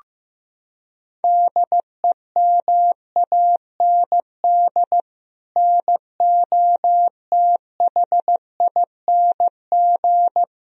Morse code, heard as a keyed tone, about 15 wpm, 700 Hz. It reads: DEMAND NOTHING
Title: Sentences from Top 1000 Words - Review Part 4 - 15wpm